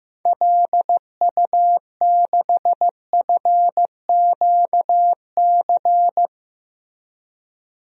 Morse code, heard as a keyed tone, about 15 wpm, 700 Hz. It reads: LU6FQC
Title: DXCC-100 Call Signs - 6 Characters - 15wpm